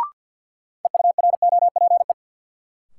Morse code, heard as a keyed tone, about 51 wpm, 700 Hz. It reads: EUROPE
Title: Sets of 5 Words Encore - New Word Drill - 50wpm